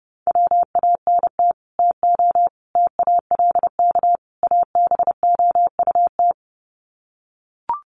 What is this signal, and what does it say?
Morse code, heard as a keyed tone, about 28 wpm, 700 Hz. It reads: WANT TO TALK ABOUT
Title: Sets of 4 Words 30wpm